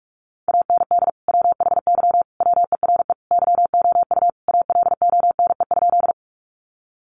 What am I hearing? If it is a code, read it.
Morse code, heard as a keyed tone, about 44 wpm, 700 Hz. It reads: AND WHY WERE YOU ALONE?
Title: Sentences from Top 1000 Words 45wpm